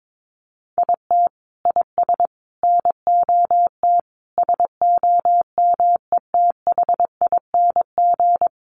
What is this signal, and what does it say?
Morse code, heard as a keyed tone, about 22 wpm, 700 Hz. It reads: IT IS NOT SOMETHING
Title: Sentences from Top 1400 Words 22wpm